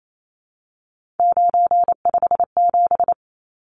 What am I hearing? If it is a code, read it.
Morse code, heard as a keyed tone, about 28 wpm, 700 Hz. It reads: 957
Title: Numbers - 3-Digit 28wpm